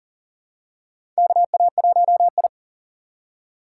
Morse code, heard as a keyed tone, about 40 wpm, 700 Hz. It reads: KA1I
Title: USA Call Signs - 4 Characters - 2x Character Spacing - 40wpm